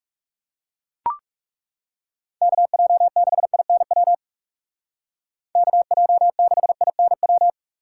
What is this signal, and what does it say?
Morse code, heard as a keyed tone, about 37 wpm, 700 Hz. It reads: KJ6INW KJ6INW
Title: Speed Racing - USA 2x3 Call Signs 30wpm